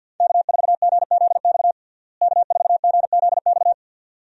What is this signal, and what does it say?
Morse code, heard as a keyed tone, about 47 wpm, 700 Hz. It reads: K4GZX K4GZX
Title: USA Call Signs - 5 Characters - Repeated 2x - 50wpm